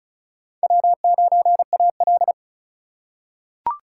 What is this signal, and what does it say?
Morse code, heard as a keyed tone, about 36 wpm, 700 Hz. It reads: W9AL
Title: USA Call Signs - 4 Characters - 4x Character Spacing - 35wpm